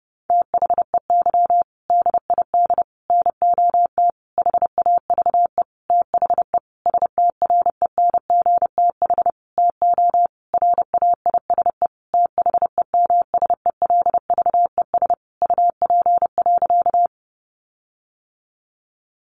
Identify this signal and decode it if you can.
Morse code, heard as a keyed tone, about 30 wpm, 700 Hz. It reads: THEY DID NOT HAVE THE STRENGTH TO RAISE THEMSELVES UP.
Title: Sentences from Top 1400 Words 30wpm